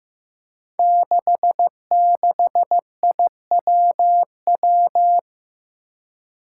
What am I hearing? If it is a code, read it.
Morse code, heard as a keyed tone, about 15 wpm, 700 Hz. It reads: 66IWW
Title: Five Challenging Letter-Number Combinations 15wpm